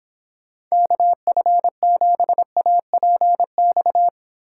Morse code, heard as a keyed tone, about 26 wpm, 700 Hz. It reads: KF7APX